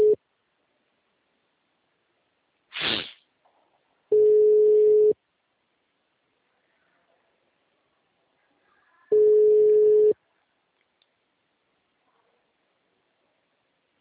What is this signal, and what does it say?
Ringback tone (Europe, 425 Hz).